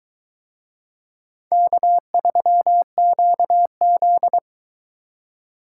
Morse code, heard as a keyed tone, about 21 wpm, 700 Hz. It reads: K3QZ